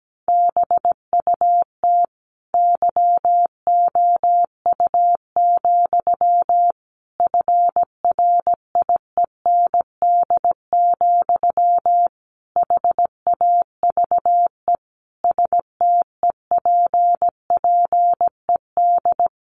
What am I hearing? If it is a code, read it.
Morse code, heard as a keyed tone, about 17 wpm, 700 Hz. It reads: BUT YOU, FRIEND, HAVE STEPPED
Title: A Sound Of Thunder 17wpm